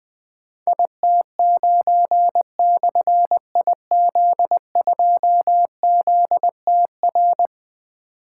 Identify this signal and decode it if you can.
Morse code, heard as a keyed tone, about 20 wpm, 700 Hz. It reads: IT9/IZ2ZTR
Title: Difficult Call Signs 20wpm